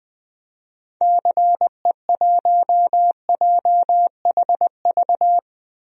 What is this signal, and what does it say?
Morse code, heard as a keyed tone, about 20 wpm, 700 Hz. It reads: CE1JHV
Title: DXCC-100 Call Signs - 6 Characters - Repeated 3x - 20wpm